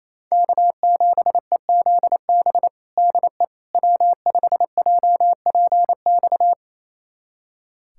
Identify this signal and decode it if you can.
Morse code, heard as a keyed tone, about 28 wpm, 700 Hz. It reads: K7EZB DE W5JPX
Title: USA Callsign DE Callsign - 5 Characters - 28wpm